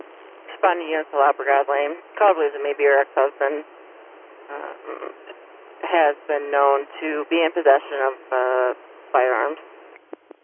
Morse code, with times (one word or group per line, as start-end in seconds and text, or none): none